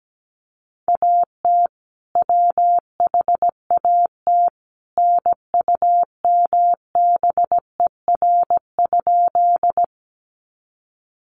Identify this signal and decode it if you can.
Morse code, heard as a keyed tone, about 17 wpm, 700 Hz. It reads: AT WHAT NUMBER?
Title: Sentences from Top 100 Words 17wpm